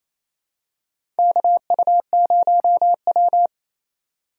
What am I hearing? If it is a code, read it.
Morse code, heard as a keyed tone, about 28 wpm, 700 Hz. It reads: KU0W
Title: USA Call Signs - 4 Characters - 4x Character Spacing - 28wpm